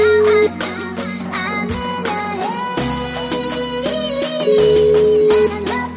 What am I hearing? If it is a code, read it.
Ringback tone (Europe, 425 Hz), with noise.